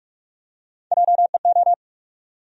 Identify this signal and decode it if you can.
Morse code, heard as a keyed tone, about 46 wpm, 700 Hz. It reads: JEO
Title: USA Call Sign Postfixes - 3 Letters 45wpm